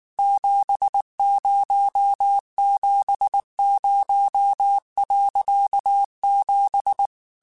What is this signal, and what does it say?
Morse code, heard as a keyed tone, about 19 wpm, 800 Hz. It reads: 7070.7
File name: QSOs / QSOs-mp3s / WA7QOX 2@18